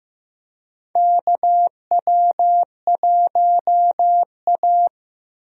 Morse code, heard as a keyed tone, about 15 wpm, 700 Hz. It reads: KW1A
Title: USA Call Signs - 4 Characters - Repeated 2x - 15wpm